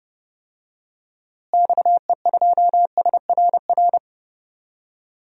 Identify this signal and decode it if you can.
Morse code, heard as a keyed tone, about 30 wpm, 700 Hz. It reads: XE2SRR
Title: DXCC-100 Call Signs - 6 Characters - Repeated 3x - 30wpm